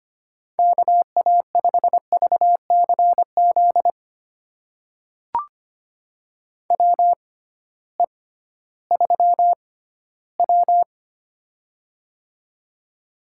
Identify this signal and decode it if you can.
Morse code, heard as a keyed tone, about 25 wpm, spaced as for 15 wpm, 700 Hz. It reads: KA5VCZ WE3W
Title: USA Call Signs - 6x Character Spacing - 25wpm